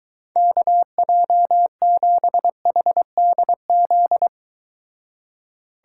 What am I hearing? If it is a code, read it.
Morse code, heard as a keyed tone, about 23 wpm, 700 Hz. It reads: KJ7HDZ